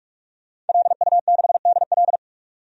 Morse code, heard as a keyed tone, about 41 wpm, 700 Hz. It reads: LU6DL
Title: DXCC-100 Call Signs - 5 Characters - 45wpm